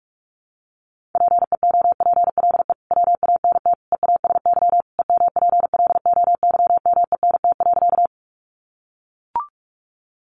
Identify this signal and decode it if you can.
Morse code, heard as a keyed tone, about 44 wpm, 700 Hz. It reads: PEOPLE WANT EASY EMPLOYMENT.
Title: Sentences from Top 1500 Words 45wpm